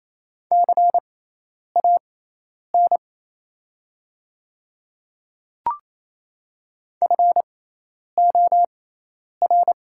Morse code, heard as a keyed tone, about 27 wpm, spaced as for 10 wpm, 700 Hz. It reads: CAN FOR
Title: 6x Character Spacing - Top 100 Words - 28wpm